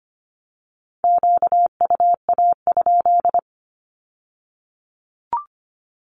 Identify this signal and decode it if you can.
Morse code, heard as a keyed tone, about 25 wpm, 700 Hz. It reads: QUA?
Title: Q-Codes with Spoken Meaning - 25wpm